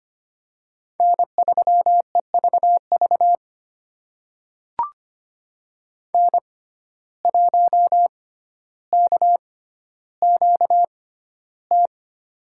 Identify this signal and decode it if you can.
Morse code, heard as a keyed tone, about 25 wpm, spaced as for 13 wpm, 700 Hz. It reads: N3EVV N1KQT
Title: USA Call Signs - 5 Characters - 6x Character Spacing - 25wpm